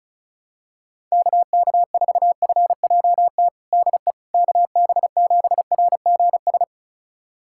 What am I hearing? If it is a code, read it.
Morse code, heard as a keyed tone, about 34 wpm, 700 Hz. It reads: KK4FJT DE KB7RGS
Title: USA Callsign DE Callsign - 6 Characters - 35wpm